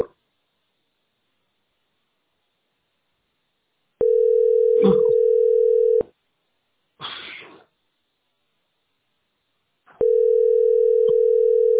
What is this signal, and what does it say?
Ringback tone (North America), 440 + 480 Hz.